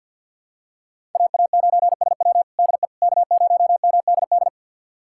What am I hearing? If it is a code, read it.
Morse code, heard as a keyed tone, about 50 wpm, 700 Hz. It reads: AA9SW DE K0MDD
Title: USA Callsign DE Callsign 50wpm